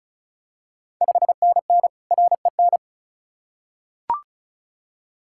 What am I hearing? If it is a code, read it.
Morse code, heard as a keyed tone, about 36 wpm, 690 Hz. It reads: 5NN REN
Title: Ontario QSO Party Exchange 35wpm